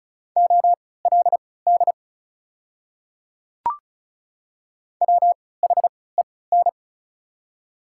Morse code, heard as a keyed tone, about 36 wpm, spaced as for 20 wpm, 700 Hz. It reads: OLD WHEN
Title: 3x Character Spacing - Top 200 Words - 35wpm